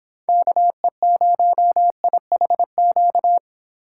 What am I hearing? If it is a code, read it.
Morse code, heard as a keyed tone, about 26 wpm, 700 Hz. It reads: KE0IHQ